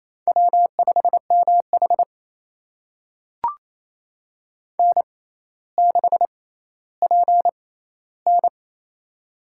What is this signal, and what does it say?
Morse code, heard as a keyed tone, about 28 wpm, spaced as for 14 wpm, 700 Hz. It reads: W5MH N6PN
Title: USA Call Signs - 4 Characters - 6x Character Spacing - 28wpm